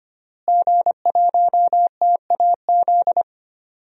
Morse code, heard as a keyed tone, about 25 wpm, 700 Hz. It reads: G1TAZ
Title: DXCC-100 Call Signs - 5 Characters - Repeated 2x - 25wpm